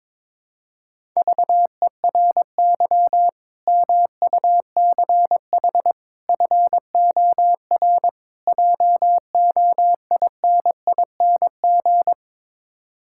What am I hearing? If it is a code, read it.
Morse code, heard as a keyed tone, about 22 wpm, 700 Hz. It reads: VERY MUCH FOR JOINING
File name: practice / 22wpm / Build A Phrase - 5-Word Phrases - 22wpm